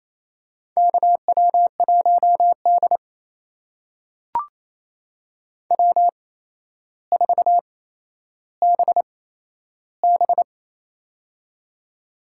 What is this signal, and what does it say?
Morse code, heard as a keyed tone, about 28 wpm, spaced as for 12 wpm, 700 Hz. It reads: KW1D W4BB
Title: USA Call Signs - 4 Characters - 8x Character Spacing - 28wpm